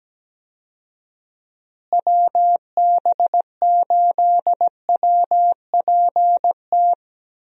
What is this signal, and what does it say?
Morse code, heard as a keyed tone, about 17 wpm, 700 Hz. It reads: WB8WPT
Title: USA Call Signs - 6 Characters - Repeated 3x - 17wpm